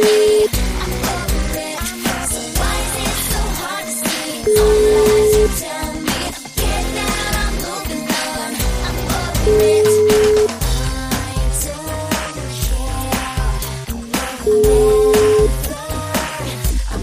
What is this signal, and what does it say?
Ringback tone (Europe, 425 Hz), with noise.